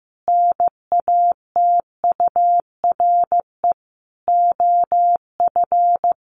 Morse code, heard as a keyed tone, about 15 wpm, 700 Hz. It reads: NATURE OF
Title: Sentences from Top 1500 Words 15wpm